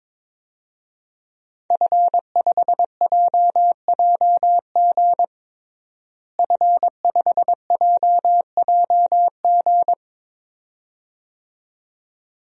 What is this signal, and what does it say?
Morse code, heard as a keyed tone, about 22 wpm, 700 Hz. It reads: F5JJG F5JJG
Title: DXCC-100 Call Signs - 5 Characters - Repeated 2x - 22wpm